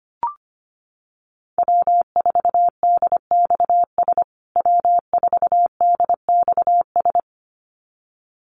Morse code, heard as a keyed tone, about 25 wpm, 700 Hz. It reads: W4DXS W4DXS
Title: USA Call Signs Repeated 2x 25wpm